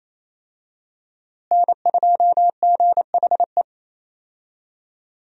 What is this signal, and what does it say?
Morse code, heard as a keyed tone, about 28 wpm, 700 Hz. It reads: N2GHE
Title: USA Call Signs - 5 Characters - Repeated 3x - 28wpm